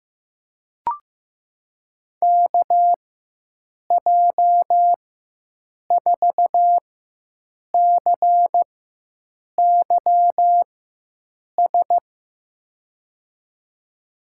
Morse code, heard as a keyed tone, about 15 wpm, spaced as for 7 wpm, 700 Hz. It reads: KJ4CYS
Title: USA Call Signs - 6 Characters - 4x Character Spacing - 15wpm